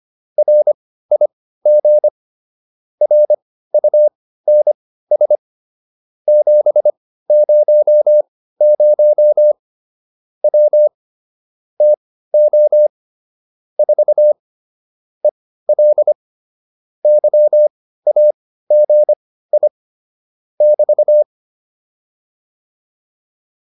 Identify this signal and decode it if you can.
Morse code, heard as a keyed tone, about 25 wpm, spaced as for 15 wpm, 600 Hz. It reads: RIG RUNS 700 W TO 4 EL YAGI <BT>